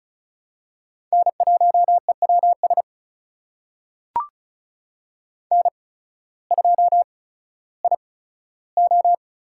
Morse code, heard as a keyed tone, about 34 wpm, spaced as for 16 wpm, 700 Hz. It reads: N1EWS N2IO
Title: USA Call Signs - 5 Characters - 8x Character Spacing - 35wpm